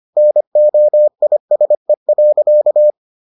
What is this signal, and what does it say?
Morse code, heard as a keyed tone, about 25 wpm, 600 Hz. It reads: NOISE.